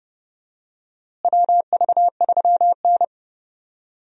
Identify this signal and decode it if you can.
Morse code, heard as a keyed tone, about 30 wpm, 700 Hz. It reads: WV3N
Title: USA Call Signs - 4 Characters - Repeated 3x - 30wpm